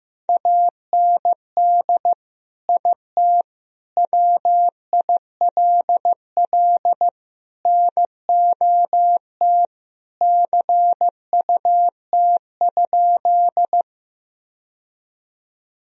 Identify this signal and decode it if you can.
Morse code, heard as a keyed tone, about 15 wpm, 700 Hz. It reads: AND IT WILL NOT CUT?
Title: Sentences from Top 1000 Words - Review Part 1 - 15wpm